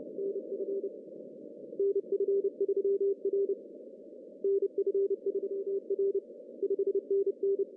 Morse code, heard as a keyed tone, about 28 wpm, 400 Hz. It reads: NF NF3R NF3R 5NN
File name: CW / cqww-cw-k5tr